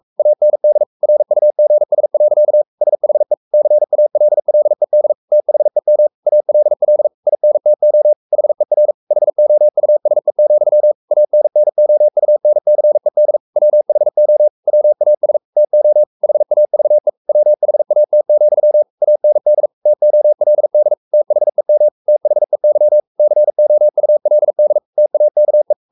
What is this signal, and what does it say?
Morse code, heard as a keyed tone, about 44 wpm, 600 Hz. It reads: AND RUGS. SHE CALLED THEM ALL INTO HER HOUSE, ANNOUNCED WHO WAS TO HAVE WHAT, AND TOLD THEM THEY COULD TAKE
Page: Romanen Never af Ken Follet på engelsk.